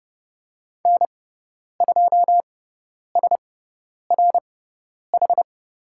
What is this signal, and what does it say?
Morse code, heard as a keyed tone, about 30 wpm, spaced as for 10 wpm, 700 Hz. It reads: N2SRH
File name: Lesson 16g - Letter c - Call Signs 30wpm